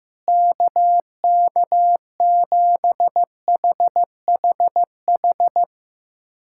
Morse code, heard as a keyed tone, about 15 wpm, 700 Hz. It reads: KK7HHH
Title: USA Callsign DE Callsign - 6 Characters - 15wpm